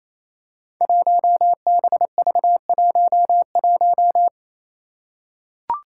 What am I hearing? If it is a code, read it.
Morse code, heard as a keyed tone, about 29 wpm, 700 Hz. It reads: 1BV11
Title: Code Groups - 5 Letter:Numbers - 28wpm